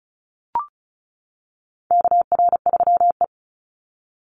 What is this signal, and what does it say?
Morse code, heard as a keyed tone, about 34 wpm, 700 Hz. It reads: KR3E
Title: Call Signs - 35wpm